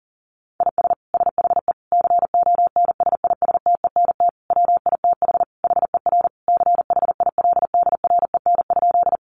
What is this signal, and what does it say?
Morse code, heard as a keyed tone, about 40 wpm, 700 Hz. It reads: IS SHE CONSISTENT WITH HER CHILDREN?
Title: Sentences from Top 1500 Words 40wpm